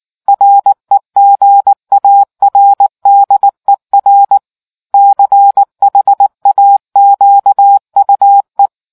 Morse code, heard as a keyed tone, about 19 wpm, 800 Hz. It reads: REGARDER CHAQUE